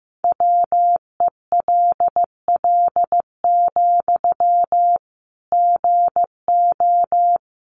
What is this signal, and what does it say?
Morse code, heard as a keyed tone, about 15 wpm, 700 Hz. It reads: WELL, GO
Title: Sentences from Top 1100 Words 15wpm